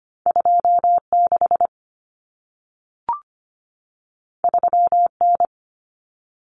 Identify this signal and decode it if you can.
Morse code, heard as a keyed tone, about 25 wpm, 700 Hz. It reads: 26 3N
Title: Common Cut Numbers - 2-Digit 25wpm